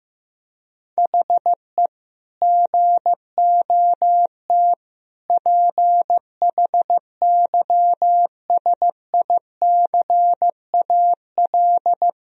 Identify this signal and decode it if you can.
Morse code, heard as a keyed tone, about 15 wpm, 700 Hz. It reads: HE GOT PHYSICAL
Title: Sentences from Top 1000 Words - Review Part 4 - 15wpm